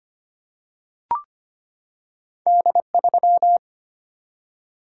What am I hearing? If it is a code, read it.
Morse code, heard as a keyed tone, about 25 wpm, 690 Hz. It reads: D3
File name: Two Letter-Number Combinations - 25wpm